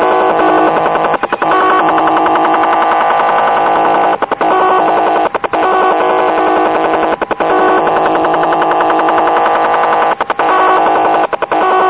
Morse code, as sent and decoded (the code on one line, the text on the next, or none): none